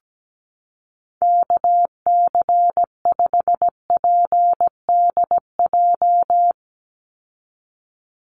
Morse code, heard as a keyed tone, about 17 wpm, 700 Hz. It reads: KC5PDJ
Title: USA Call Signs 17wpm